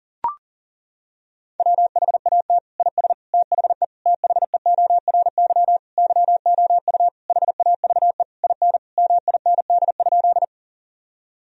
Morse code, heard as a keyed tone, about 40 wpm, 700 Hz. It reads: WHAT IS THE THEORY YOU HAVE IN MIND?
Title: Sentences from Top 1000 Words - Review Part 3 - 40wpm